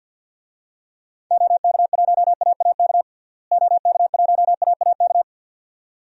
Morse code, heard as a keyed tone, about 52 wpm, 700 Hz. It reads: OK1AAK OK1AAK
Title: DXCC-100 Call Signs - 6 Characters - Repeated 2x - 50wpm